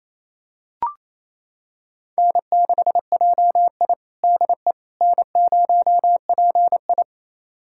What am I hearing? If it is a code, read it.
Morse code, heard as a keyed tone, about 28 wpm, 700 Hz. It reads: N6JI DE N0PI
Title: USA Callsign DE Callsign - 4 Characters - 28wpm